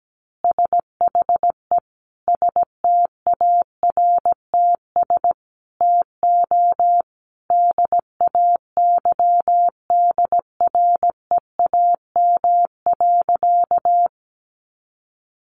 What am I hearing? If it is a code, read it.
Morse code, heard as a keyed tone, about 17 wpm, 700 Hz. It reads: SHE STARTS TO DAYDREAM.
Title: Easy Phrases 17wpm